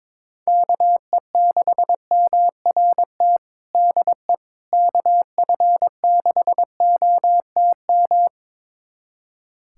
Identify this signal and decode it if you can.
Morse code, heard as a keyed tone, about 22 wpm, 700 Hz. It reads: KE6MRT DE KF6OTM
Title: USA Callsign DE Callsign - 6 Characters - 22wpm